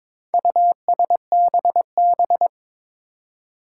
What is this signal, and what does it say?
Morse code, heard as a keyed tone, about 22 wpm, 700 Hz. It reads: USBB
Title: Four Challenging Letter-Number Combinations 22wpm